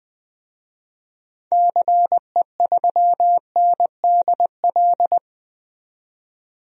Morse code, heard as a keyed tone, about 20 wpm, 700 Hz. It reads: CE3NDL
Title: DXCC-100 Call Signs - 6 Characters - 20wpm